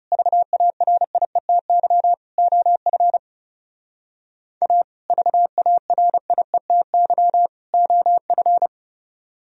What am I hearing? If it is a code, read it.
Morse code, heard as a keyed tone, about 35 wpm, 690 Hz. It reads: VARIETY OF A VARIETY OF
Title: Speed-Racing - Sets of 3 Words 30wpm